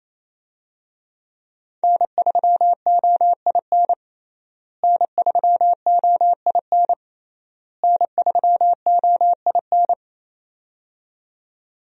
Morse code, heard as a keyed tone, about 28 wpm, 700 Hz. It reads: N3OIN N3OIN N3OIN
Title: USA Call Signs - 5 Characters - Repeated 3x - 28wpm